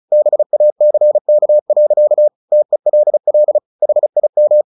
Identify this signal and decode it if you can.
Morse code, heard as a keyed tone, about 34 wpm, 600 Hz. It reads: BACK. TELL HIM
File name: the old man and the sea 2 35wpm